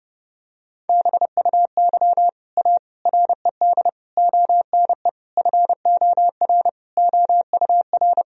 Morse code, heard as a keyed tone, about 30 wpm, 700 Hz. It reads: BUY A RED ONE FOR OUR
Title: Sentences from Top 1000 Words - Review Part 2 - 30wpm